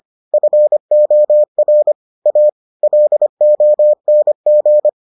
Morse code, heard as a keyed tone, about 25 wpm, 600 Hz. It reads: FOR A LONG